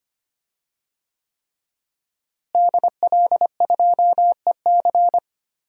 Morse code, heard as a keyed tone, about 25 wpm, 700 Hz. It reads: DL2EC
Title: DXCC-100 Call Signs - 5 Characters - Repeated 3x - 25wpm